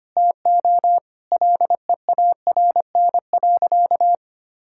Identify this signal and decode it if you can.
Morse code, heard as a keyed tone, about 25 wpm, 700 Hz. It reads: TO LEARN.
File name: practice / 25wpm / Sentences with 4-Word Phrases - 25wpm